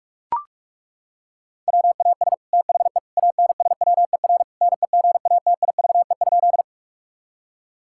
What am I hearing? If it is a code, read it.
Morse code, heard as a keyed tone, about 41 wpm, 700 Hz. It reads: WAS THE ANSWER NEGATIVE?
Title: Sentences from Top 1000 Words - Review Part 6 - 45wpm